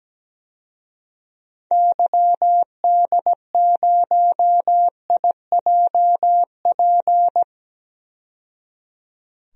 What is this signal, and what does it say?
Morse code, heard as a keyed tone, about 17 wpm, 700 Hz. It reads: YD0IJP
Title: DXCC-100 Call Signs - 6 Characters - 17wpm